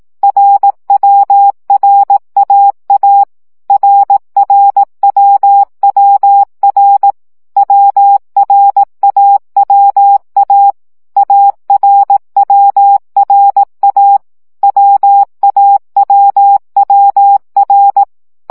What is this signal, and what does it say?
Morse code, heard as a keyed tone, about 18 wpm, 800 Hz. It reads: RWRAA RRWWR WRAWA ARWRA WAWWR
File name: CW gyakorlas a r w 15 baud